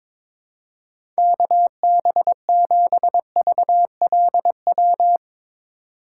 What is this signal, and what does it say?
Morse code, heard as a keyed tone, about 22 wpm, 700 Hz. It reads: KB7VLW
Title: USA Call Signs - 6 Characters - 4x Character Spacing - 22wpm